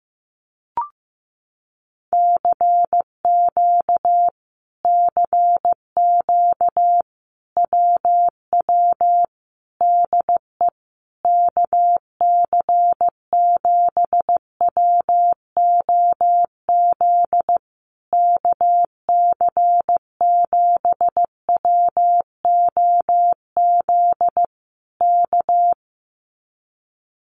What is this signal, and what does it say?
Morse code, heard as a keyed tone, about 15 wpm, 700 Hz. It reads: CQ CQ WW DE KC7WOZ KC7WOZ K
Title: Common QSO Phrases Encore 15wpm